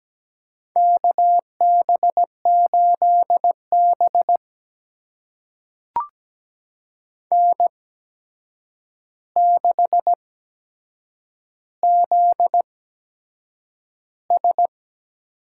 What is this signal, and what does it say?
Morse code, heard as a keyed tone, about 17 wpm, spaced as for 7 wpm, 700 Hz. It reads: KB8B N6ZS
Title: USA Call Signs - 4 Characters - 8x Character Spacing - 17wpm